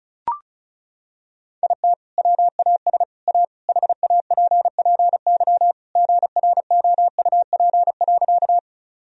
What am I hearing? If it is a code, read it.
Morse code, heard as a keyed tone, about 34 wpm, 700 Hz. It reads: IT WAS A HAPPY GROUP.
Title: Sentences from Top 1000 Words - Review Part 1 - 35wpm